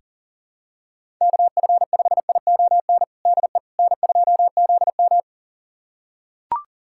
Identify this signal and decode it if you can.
Morse code, heard as a keyed tone, about 40 wpm, 700 Hz. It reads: KF5ION DE N2ZM
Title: USA Callsign DE Callsign 40wpm